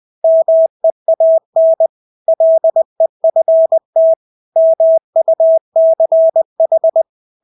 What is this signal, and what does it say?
Morse code, heard as a keyed tone, about 20 wpm, 650 Hz. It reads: MEAN LEFT MUCH
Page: Engelske ord 20 wpm